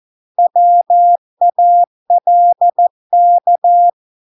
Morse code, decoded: WALK

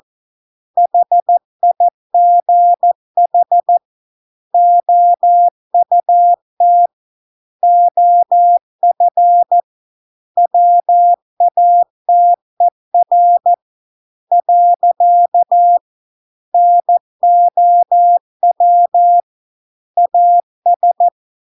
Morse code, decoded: HIGH OUT OF WATER . NOW AS

